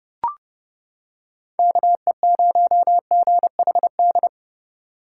Morse code, heard as a keyed tone, about 30 wpm, 700 Hz. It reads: KE0GHD